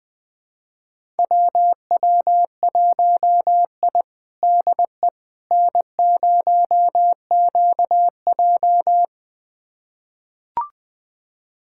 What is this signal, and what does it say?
Morse code, heard as a keyed tone, about 19 wpm, 700 Hz. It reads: WW1I DE N0QJ
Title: USA Callsign DE Callsign - 4 Characters - 20wpm